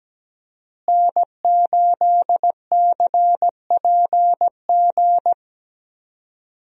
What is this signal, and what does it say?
Morse code, heard as a keyed tone, about 17 wpm, 700 Hz. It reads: N8CPG
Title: USA Call Signs - 5 Characters - Repeated 3x - 17wpm